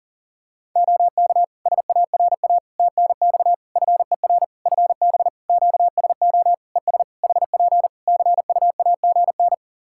Morse code, heard as a keyed tone, about 40 wpm, 700 Hz. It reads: OK SARA TNX FER FB QSO ES HP CUAGN
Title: Practice QSO - Exchange 4 - Ending - 40wpm